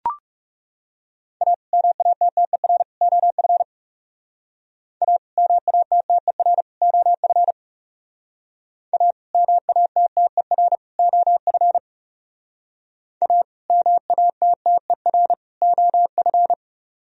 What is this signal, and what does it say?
Morse code, heard as a keyed tone, about 37 wpm, 700 Hz. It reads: A MATTER OF A MATTER OF A MATTER OF A MATTER OF
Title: Speed-Racing - Sets of 3 Words 30wpm